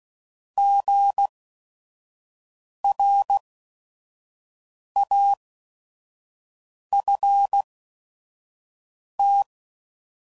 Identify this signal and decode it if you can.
Morse code, heard as a keyed tone, about 16 wpm, spaced as for 5 wpm, 800 Hz. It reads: GRAFT
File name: Prefixes / By First Letter / Gs